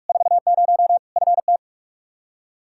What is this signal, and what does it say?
Morse code, heard as a keyed tone, about 46 wpm, 700 Hz. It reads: 40 FT
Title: K1USN SST QSO Practice 45wpm